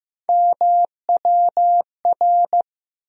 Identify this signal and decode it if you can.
Morse code, heard as a keyed tone, about 15 wpm, 700 Hz. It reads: MWR
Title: USA Call Sign Postfixes - 3 Letters 15wpm